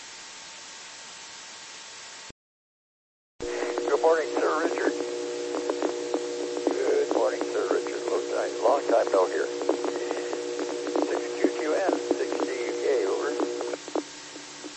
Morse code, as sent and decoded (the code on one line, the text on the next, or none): none